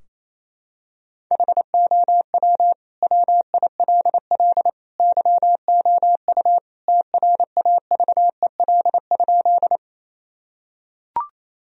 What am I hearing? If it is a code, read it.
Morse code, heard as a keyed tone, about 28 wpm, 700 Hz. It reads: HOW WILL YOU TRAVEL?
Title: Sentences from Top 1000 Words - Review Part 2 - 28wpm